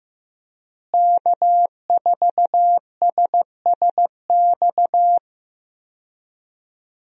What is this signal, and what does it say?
Morse code, heard as a keyed tone, about 15 wpm, 700 Hz. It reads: K4SSX
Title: USA Call Signs - 5 Characters - Repeated 3x - 15wpm